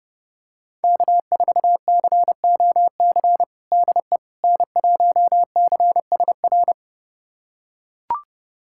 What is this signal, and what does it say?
Morse code, heard as a keyed tone, about 31 wpm, 700 Hz. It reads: K4COC DE N1CSR
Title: USA Callsign DE Callsign 30wpm